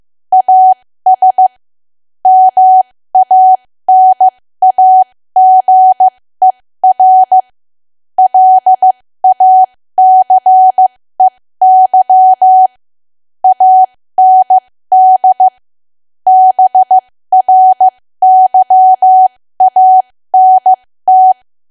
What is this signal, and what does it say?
Morse code, heard as a keyed tone, about 15 wpm, spaced as for 13 wpm, 750 Hz. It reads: AS MANAGER LACEY AND BRYANT
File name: cwa / W1AW-Level2 / 13WPM